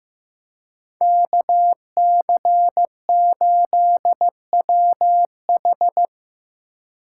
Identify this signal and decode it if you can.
Morse code, heard as a keyed tone, about 15 wpm, 700 Hz. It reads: KC8WH
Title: K1USN SST QSO Practice - 15wpm